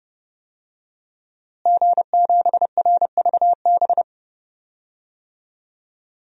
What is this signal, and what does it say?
Morse code, heard as a keyed tone, about 30 wpm, 700 Hz. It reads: G7RVB
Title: DXCC-100 Call Signs - 5 Characters - Repeated 2x - 30wpm